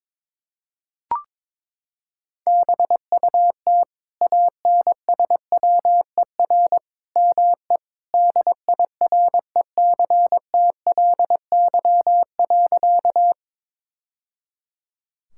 Morse code, heard as a keyed tone, about 22 wpm, 700 Hz. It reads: BUT ANSWER ME DIRECTLY.
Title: Sentences from Top 1000 Words - Review Part 5 - 22wpm